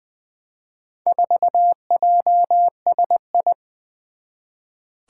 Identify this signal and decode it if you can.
Morse code, heard as a keyed tone, about 20 wpm, 700 Hz. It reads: 4JSI
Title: Four Challenging Letter-Number Combinations 20wpm